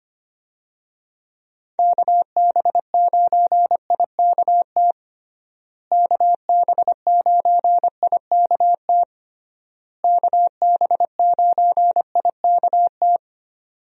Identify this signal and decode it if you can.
Morse code, heard as a keyed tone, about 25 wpm, 700 Hz. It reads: KB9IKT KB9IKT KB9IKT
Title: USA Call Signs - 6 Characters - Repeated 3x - 25wpm